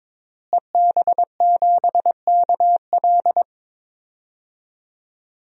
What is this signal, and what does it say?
Morse code, heard as a keyed tone, about 22 wpm, 700 Hz. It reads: EB7KL
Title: DXCC-100 Call Signs - 5 Characters - Repeated 2x - 22wpm